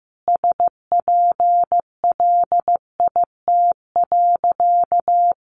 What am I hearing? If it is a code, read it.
Morse code, heard as a keyed tone, about 15 wpm, 700 Hz. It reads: SPLIT.
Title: Easy Phrases 15wpm